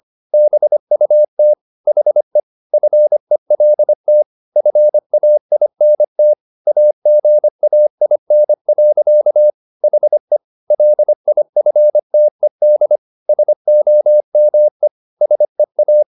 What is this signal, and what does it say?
Morse code, heard as a keyed tone, about 25 wpm, 600 Hz. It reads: BUT HE FELT FAINT AGAIN. HE LIFTED SOME SEA